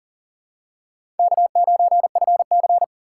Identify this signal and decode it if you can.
Morse code, heard as a keyed tone, about 40 wpm, 700 Hz. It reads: K9FC
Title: USA Call Signs - 4 Characters - 3x Character Spacing - 40wpm